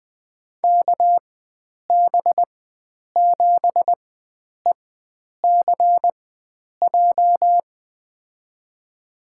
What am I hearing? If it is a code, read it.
Morse code, heard as a keyed tone, about 21 wpm, spaced as for 9 wpm, 700 Hz. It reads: KB7ECJ
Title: USA Call Signs - 4x Character Spacing - 20wpm